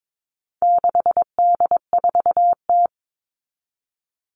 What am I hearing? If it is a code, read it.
Morse code, heard as a keyed tone, about 22 wpm, 700 Hz. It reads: 6D4T
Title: All Cut Numbers - 4-Digit 22wpm